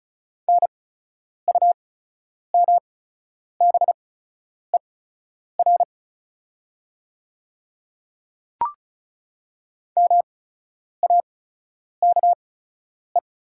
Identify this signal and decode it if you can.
Morse code, heard as a keyed tone, about 36 wpm, spaced as for 10 wpm, 690 Hz. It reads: NUMBER MAKE
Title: 8x Character Spacing - Top 300 Words - 35wpm